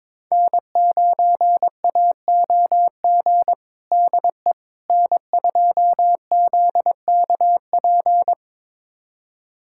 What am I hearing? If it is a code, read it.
Morse code, heard as a keyed tone, about 22 wpm, 700 Hz. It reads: N9AOG DE N2ZKP
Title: USA Callsign DE Callsign - 5 Characters - 22wpm